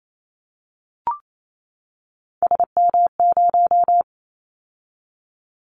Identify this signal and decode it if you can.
Morse code, heard as a keyed tone, about 28 wpm, 700 Hz. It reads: SM0
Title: Callsign Prefixes for Top 25 DXCC - 28wpm